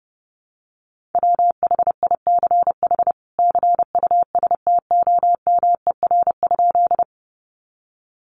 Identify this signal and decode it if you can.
Morse code, heard as a keyed tone, about 30 wpm, 700 Hz. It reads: WHICH CUSTOMER?
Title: Sentences from Top 1100 Words 30wpm